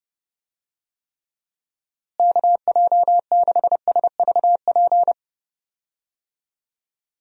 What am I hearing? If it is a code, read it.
Morse code, heard as a keyed tone, about 30 wpm, 700 Hz. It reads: KJ6SVP
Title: USA Call Signs - 6 Characters - Repeated 2x - 30wpm